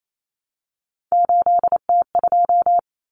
Morse code, heard as a keyed tone, about 28 wpm, 700 Hz. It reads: 8T2
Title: Common Cut Numbers - 3-Digit 28wpm